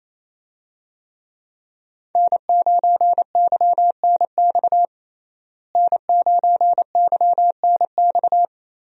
Morse code, heard as a keyed tone, about 28 wpm, 700 Hz. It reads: N9YNX N9YNX
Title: USA Call Signs - 5 Characters - Repeated 3x - 28wpm